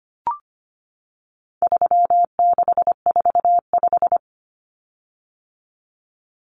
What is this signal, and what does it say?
Morse code, heard as a keyed tone, about 25 wpm, 700 Hz. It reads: 3645
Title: Common Cut Numbers - 4-Digit 25wpm